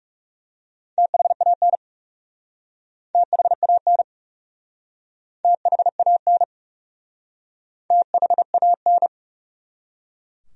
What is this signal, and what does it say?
Morse code, heard as a keyed tone, about 40 wpm, 700 Hz. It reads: THAN THAN THAN THAN